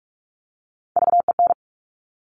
Morse code, heard as a keyed tone, about 46 wpm, 700 Hz. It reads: VEN
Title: Top Three Letter Combinations in Words 45wpm